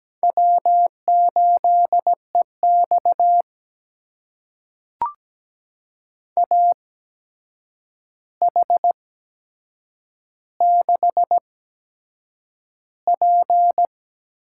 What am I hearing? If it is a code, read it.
Morse code, heard as a keyed tone, about 17 wpm, spaced as for 7 wpm, 700 Hz. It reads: W8EX AH6P
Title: USA Call Signs - 4 Characters - 8x Character Spacing - 17wpm